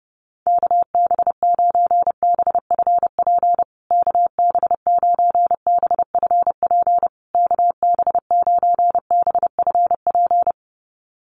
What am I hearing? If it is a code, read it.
Morse code, heard as a keyed tone, about 30 wpm, 700 Hz. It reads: KB9BFP KB9BFP KB9BFP
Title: USA Call Signs Repeated 3x 30wpm